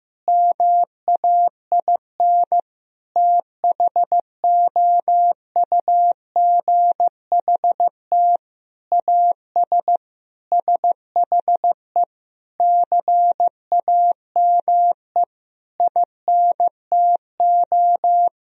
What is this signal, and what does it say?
Morse code, heard as a keyed tone, about 15 wpm, 700 Hz. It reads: MAIN THOUGHT AS SHE CAME INTO
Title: Sentences from Top 1000 Words - Review Part 2 - 15wpm